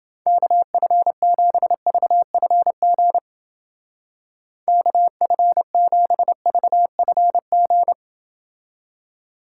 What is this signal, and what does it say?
Morse code, heard as a keyed tone, about 28 wpm, 700 Hz. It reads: KF7VFG KF7VFG